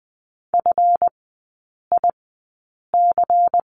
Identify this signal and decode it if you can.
Morse code, heard as a keyed tone, about 20 wpm, spaced as for 8 wpm, 700 Hz. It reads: FIC
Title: 198 Lesson 32e - Letter k - Three Characters 20wpm